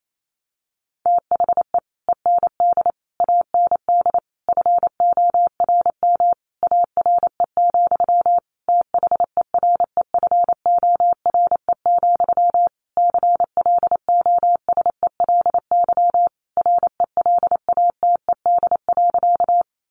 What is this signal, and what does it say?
Morse code, heard as a keyed tone, about 28 wpm, 700 Hz. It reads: THE END AND FORM ARE, THEREFORE, CLOSELY RELATED.
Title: Sentences from Top 1400 Words 28wpm